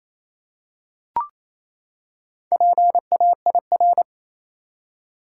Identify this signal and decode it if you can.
Morse code, heard as a keyed tone, about 28 wpm, 700 Hz. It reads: PAIR
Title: Four Letter Words - 28wpm